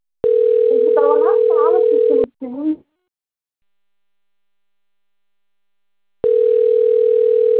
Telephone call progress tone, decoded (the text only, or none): ringback tone